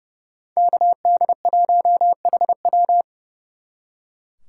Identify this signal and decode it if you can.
Morse code, heard as a keyed tone, about 35 wpm, 700 Hz. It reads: KD1HW
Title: Speed Racing - USA Call Signs 30wpm